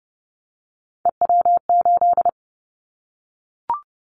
Morse code, heard as a keyed tone, about 30 wpm, 700 Hz. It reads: EW8
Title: Callsign Prefixes for Top 50 DXCC - 30wpm